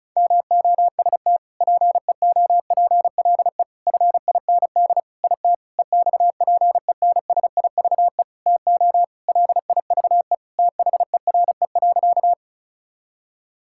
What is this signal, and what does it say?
Morse code, heard as a keyed tone, about 34 wpm, 700 Hz. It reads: MOST PEOPLE FIND IT EXPENSIVE TO LIVE THERE.
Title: Sentences from Top 1000 Words - Review Part 6 - 35wpm